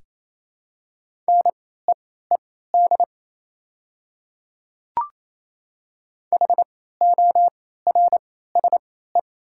Morse code, heard as a keyed tone, about 28 wpm, spaced as for 16 wpm, 700 Hz. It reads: NEED HORSE
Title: 3x Character Spacing - Top 500 Words - 28wpm